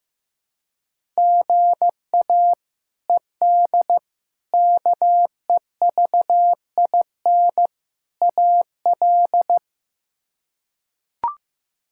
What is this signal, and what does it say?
Morse code, heard as a keyed tone, about 15 wpm, 700 Hz. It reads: GA ED KEVIN AL
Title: K1USN SST QSO Practice - 15wpm